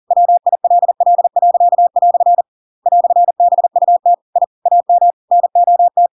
Morse code, heard as a keyed tone, about 40 wpm, 700 Hz. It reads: WILL." "BUT I AM NOT